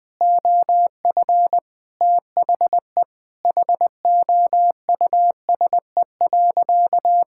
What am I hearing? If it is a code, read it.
Morse code, heard as a keyed tone, about 20 wpm, 700 Hz. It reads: OF THE HOUSE.
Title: Sentences from Top 1600 Words 20wpm